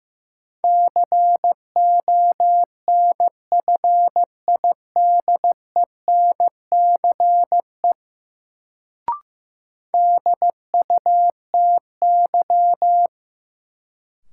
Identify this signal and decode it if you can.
Morse code, heard as a keyed tone, about 15 wpm, 700 Hz. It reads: CONFIDENCE DUTY
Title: Sentences from Top 1500 Words - New Word Drill - 15wpm